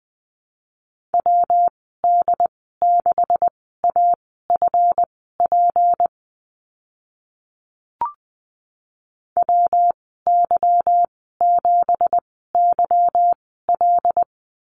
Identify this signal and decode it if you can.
Morse code, heard as a keyed tone, about 20 wpm, spaced as for 15 wpm, 700 Hz. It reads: WD6AFP WY7YL
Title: 173 Lesson 28g - Number 9 - Call Signs 20wpm